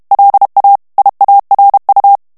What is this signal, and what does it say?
Morse code, heard as a keyed tone, about 32 wpm, 800 Hz. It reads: LA IARU
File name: morse mp3 / ESPAGNOL TEXTE EN CLAIR 1 VITESSE 30 MOTS MINUTES